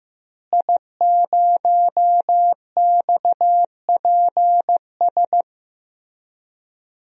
Morse code, heard as a keyed tone, about 15 wpm, 700 Hz. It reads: I0XPS
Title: DXCC-100 Call Signs - 5 Characters - Repeated 2x - 15wpm